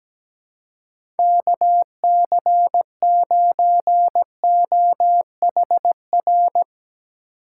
Morse code, heard as a keyed tone, about 17 wpm, 700 Hz. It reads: KC9OHR
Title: USA Call Signs - 6 Characters - 2x Character Spacing - 17wpm